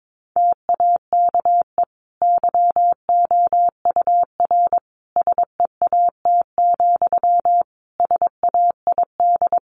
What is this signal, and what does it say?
Morse code, heard as a keyed tone, about 22 wpm, 700 Hz. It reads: TAKE YOUR SEAT, SAID
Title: Sentences from Top 1500 Words 22wpm